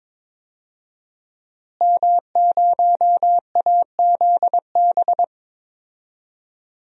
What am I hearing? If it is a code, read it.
Morse code, heard as a keyed tone, about 22 wpm, 700 Hz. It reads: M0AZB
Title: DXCC-100 Call Signs - 5 Characters - Repeated 2x - 22wpm